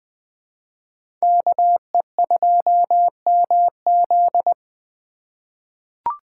Morse code, heard as a keyed tone, about 20 wpm, 700 Hz. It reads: KE2MZ
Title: USA Call Signs - 5 Characters - 2x Character Spacing - 20wpm